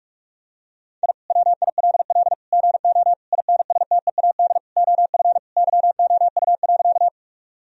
Morse code, heard as a keyed tone, about 46 wpm, 700 Hz. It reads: I WILL GO INSTEAD OF YOU.
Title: Sentences from Top 1000 Words - Review Part 4 - 45wpm